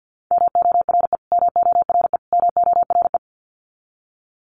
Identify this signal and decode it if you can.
Morse code, heard as a keyed tone, about 48 wpm, 700 Hz. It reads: MORE MORE MORE
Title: Top 200 Words Repeated 50wpm